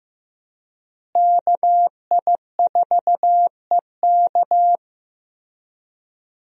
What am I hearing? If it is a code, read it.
Morse code, heard as a keyed tone, about 15 wpm, 700 Hz. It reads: KI4EK
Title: USA Call Signs - 5 Characters - Repeated 3x - 15wpm